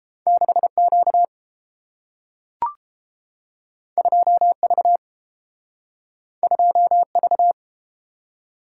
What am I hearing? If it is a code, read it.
Morse code, heard as a keyed tone, about 30 wpm, 700 Hz. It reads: 6Q 2V 2V